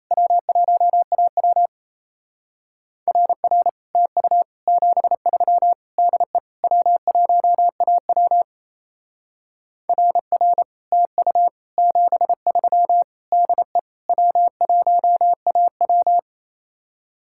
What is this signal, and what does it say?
Morse code, heard as a keyed tone, about 34 wpm, 700 Hz. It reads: W1AW RR TU 73 DE W1AW RR TU 73 DE W1AW
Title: Speed-Racing Common QSO Phrases - 28wpm